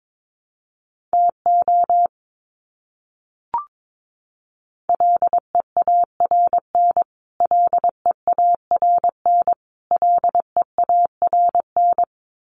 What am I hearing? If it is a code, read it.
Morse code, heard as a keyed tone, about 22 wpm, 700 Hz. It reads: TO LEARN LEARN LEARN
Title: Top 500 Words Repeated 22wpm